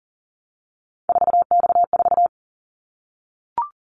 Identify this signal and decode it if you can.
Morse code, heard as a keyed tone, about 40 wpm, 700 Hz. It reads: 4X4
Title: Callsign Prefixes for Top 50 DXCC - 40wpm